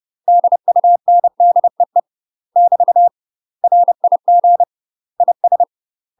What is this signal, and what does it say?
Morse code, heard as a keyed tone, about 30 wpm, 700 Hz. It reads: DUNDEE <BT> RIG IS
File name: UK MASTER QSOs TRAINING / 30WPM / UK MASTER QSOs 0047